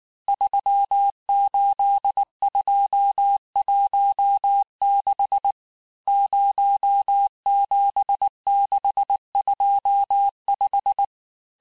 Morse code, heard as a keyed tone, about 19 wpm, 800 Hz. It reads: 38216 07625
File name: cours-du-mercredi / audios / cours-869-texte-CHIFFRE-vitesse-18-MPM